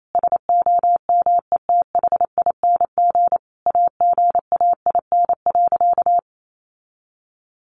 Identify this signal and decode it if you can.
Morse code, heard as a keyed tone, about 28 wpm, 700 Hz. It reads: SOMETHING AGAIN.
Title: Sentences from Top 1100 Words 28wpm